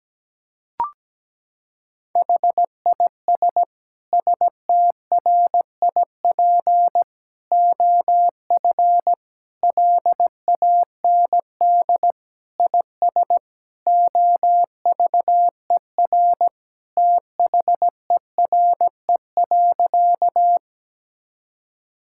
Morse code, heard as a keyed tone, about 17 wpm, 700 Hz. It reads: HIS STRIP OF LAND IS OVER THERE.
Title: Sentences from Top 1600 Words 17wpm